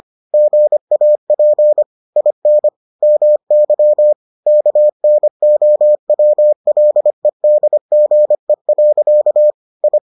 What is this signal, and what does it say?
Morse code, heard as a keyed tone, about 25 wpm, 600 Hz. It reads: GAP IN MY KNOWLEDGE. I